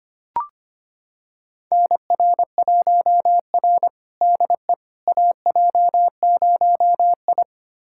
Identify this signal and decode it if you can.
Morse code, heard as a keyed tone, about 25 wpm, 700 Hz. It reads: NR1R DE AJ0I
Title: USA Callsign DE Callsign - 4 Characters - 25wpm